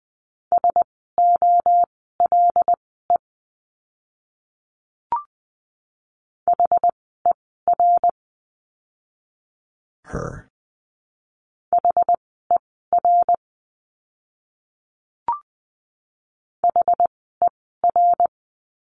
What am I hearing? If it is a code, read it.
Morse code, heard as a keyed tone, about 20 wpm, spaced as for 15 wpm, 700 Hz. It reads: SOLE HER HER HER
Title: 075 Lesson 13f - Letter L - Words 20wpm